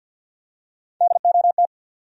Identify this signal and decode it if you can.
Morse code, heard as a keyed tone, about 53 wpm, 700 Hz. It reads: DOT
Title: USA Common Names - Max 5 Letters - 50wpm